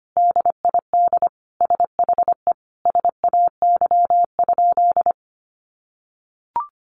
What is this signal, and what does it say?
Morse code, heard as a keyed tone, about 25 wpm, 700 Hz. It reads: DID SHE SAY?
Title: Sentences from Top 500 Words 25wpm